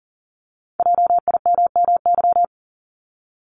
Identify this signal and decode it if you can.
Morse code, heard as a keyed tone, about 41 wpm, 700 Hz. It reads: JIMMY
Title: Multi-letter Phonemes 40wpm